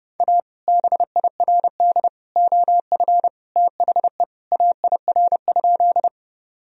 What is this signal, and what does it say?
Morse code, heard as a keyed tone, about 32 wpm, 700 Hz. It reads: A BIRD OF THE AIR?
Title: Sentences from Top 1000 Words - Review Part 2 - 30wpm